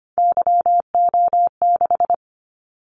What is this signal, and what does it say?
Morse code, heard as a keyed tone, about 25 wpm, 700 Hz. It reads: YO6
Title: Callsign Prefixes for Top 25 DXCC - 25wpm